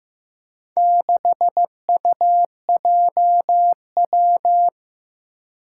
Morse code, heard as a keyed tone, about 15 wpm, 700 Hz. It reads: 6UJW
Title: Four Challenging Letter-Number Combinations 15wpm